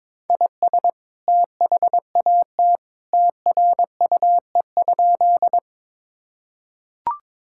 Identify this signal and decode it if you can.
Morse code, heard as a keyed tone, about 23 wpm, 700 Hz. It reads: IS THAT TRUE?
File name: practice / 22wpm / Sentences with 2-Word Phrases - 22wpm